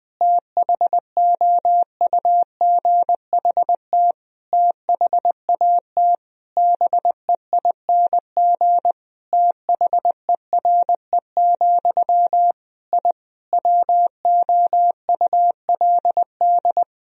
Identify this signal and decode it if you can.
Morse code, heard as a keyed tone, about 20 wpm, 700 Hz. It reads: THOUGHT THAT BEING THERE, I WOULD
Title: Sentences from Top 1000 Words - Review Part 3 - 20wpm